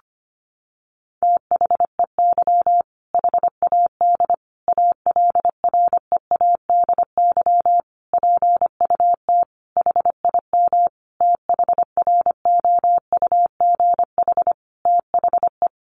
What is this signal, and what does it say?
Morse code, heard as a keyed tone, about 25 wpm, 700 Hz. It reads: THEY HAD ALREADY PUT HIM THROUGH THE
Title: Sentences from Top 1200 Words 25wpm